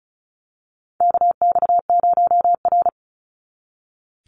Morse code, heard as a keyed tone, about 34 wpm, 700 Hz. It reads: KX0R
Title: NA SOTA Activator Callsigns 35wpm